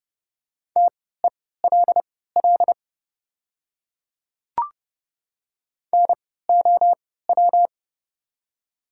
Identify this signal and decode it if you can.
Morse code, heard as a keyed tone, about 30 wpm, spaced as for 17 wpm, 690 Hz. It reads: TELL NOW
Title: 3x Character Spacing - Top 200 Words - 30wpm